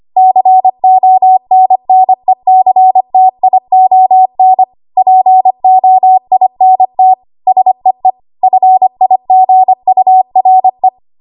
Morse code, heard as a keyed tone, about 25 wpm, 750 Hz. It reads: CONNECTION POINT SEE FIGURE
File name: Archive / 25 WPM / 25WPM